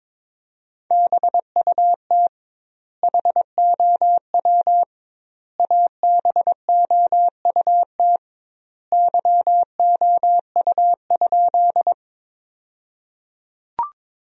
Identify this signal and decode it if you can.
Morse code, heard as a keyed tone, about 22 wpm, 700 Hz. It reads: BUT HOW ABOUT YOU?
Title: 2x Word-Spacing - Sentences from Top 100 Words - 22wpm